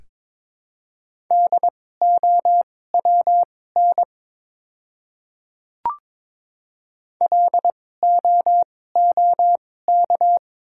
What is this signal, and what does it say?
Morse code, heard as a keyed tone, about 22 wpm, spaced as for 16 wpm, 700 Hz. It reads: DOWN LOOK
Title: 2x Character Spacing - Top 100 Words - 22wpm